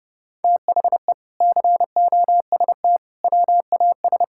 Morse code, heard as a keyed tone, about 31 wpm, 700 Hz. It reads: THE COST WAS
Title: Sentences from Top 1000 Words - Review Part 6 - 30wpm